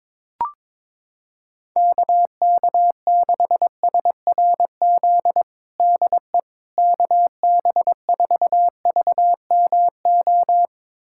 Morse code, heard as a keyed tone, about 21 wpm, 700 Hz. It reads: KK6SRZ DE KB4VMO
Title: USA Callsign DE Callsign - 6 Characters - 22wpm